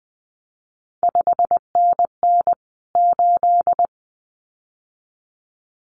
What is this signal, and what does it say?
Morse code, heard as a keyed tone, about 20 wpm, 700 Hz. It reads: 5NN 8
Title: CQ WW DX Contest Exchange Encore - 20wpm